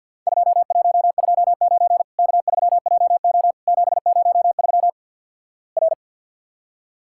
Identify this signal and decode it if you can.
Morse code, heard as a keyed tone, about 49 wpm, 700 Hz. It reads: 2129 K3JO 703 R
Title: Worked All Europe DX Contest - QTC Exchange - 50wpm